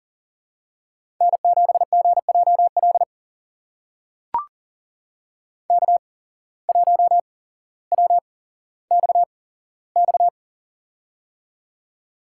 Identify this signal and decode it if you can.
Morse code, heard as a keyed tone, about 40 wpm, spaced as for 17 wpm, 700 Hz. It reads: N7GJL K1WXX
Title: USA Call Signs - 5 Characters - 8x Character Spacing - 40wpm